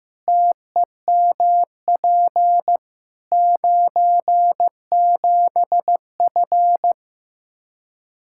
Morse code, heard as a keyed tone, about 15 wpm, 700 Hz. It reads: TEMP 97F
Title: Practice QSO - Exchange 2 - Info - 15wpm